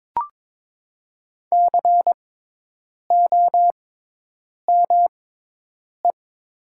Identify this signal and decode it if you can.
Morse code, heard as a keyed tone, about 22 wpm, spaced as for 8 wpm, 700 Hz. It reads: COME
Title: 6x Character Spacing - Top 100 Words - 22wpm